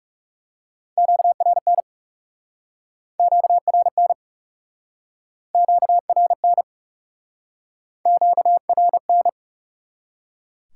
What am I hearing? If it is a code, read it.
Morse code, heard as a keyed tone, about 40 wpm, 700 Hz. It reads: QRN QRN QRN QRN